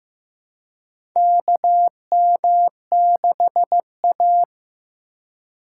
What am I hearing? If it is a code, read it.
Morse code, heard as a keyed tone, about 15 wpm, 700 Hz. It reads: KM6A
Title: USA Call Signs - 4 Characters - 3x Character Spacing - 15wpm